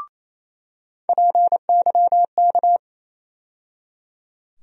Morse code, heard as a keyed tone, about 28 wpm, 700 Hz. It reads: PYK
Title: USA Call Sign Postfixes - 3 Letters 28wpm